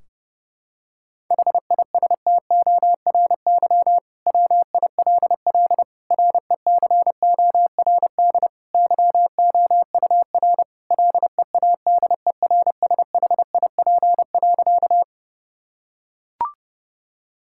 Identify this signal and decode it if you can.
Morse code, heard as a keyed tone, about 32 wpm, 700 Hz. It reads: HISTORY WILL RECORD YOUR LEADERSHIP.
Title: Sentences from Top 1600 Words 30wpm